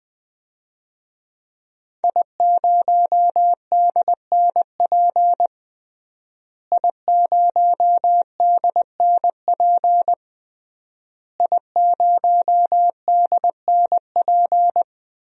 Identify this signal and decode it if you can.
Morse code, heard as a keyed tone, about 21 wpm, 700 Hz. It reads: I0DNP I0DNP I0DNP
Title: DXCC-100 Call Signs - 5 Characters - Repeated 3x - 20wpm